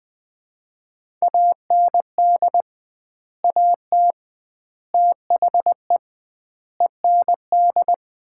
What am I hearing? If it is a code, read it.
Morse code, heard as a keyed tone, about 20 wpm, 700 Hz. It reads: AND AT THE END
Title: 2x Word-Spacing - Sets of 5 Words - 20wpm